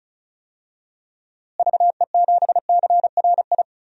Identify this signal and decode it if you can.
Morse code, heard as a keyed tone, about 36 wpm, 700 Hz. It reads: VE7CRI
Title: DXCC-100 Call Signs - 6 Characters - Repeated 3x - 35wpm